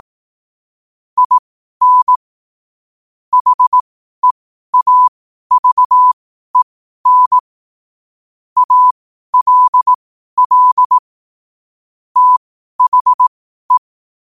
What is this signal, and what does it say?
Morse code, heard as a keyed tone, about 18 wpm, spaced as for 12 wpm, 1000 Hz. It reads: IN HEAVEN ALL THE
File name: qotdfiles / 12wpm